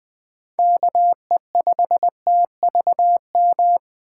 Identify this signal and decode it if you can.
Morse code, heard as a keyed tone, about 20 wpm, 700 Hz. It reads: KE5TVM
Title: USA Call Signs - 6 Characters - 2x Character Spacing - 20wpm